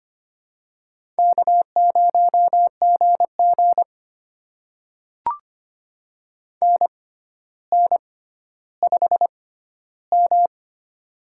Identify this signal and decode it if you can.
Morse code, heard as a keyed tone, about 25 wpm, spaced as for 13 wpm, 700 Hz. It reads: K0GG NN5M
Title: USA Call Signs - 4 Characters - 6x Character Spacing - 25wpm